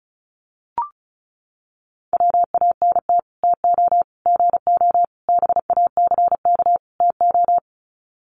Morse code, heard as a keyed tone, about 34 wpm, 700 Hz. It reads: WANT TO GO BACK TO
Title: Sets of 5 Words 35wpm